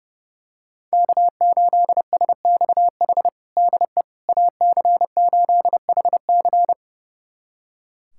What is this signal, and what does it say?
Morse code, heard as a keyed tone, about 30 wpm, 700 Hz. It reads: K8SXH DE AC8HC
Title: USA Callsign DE Callsign 30wpm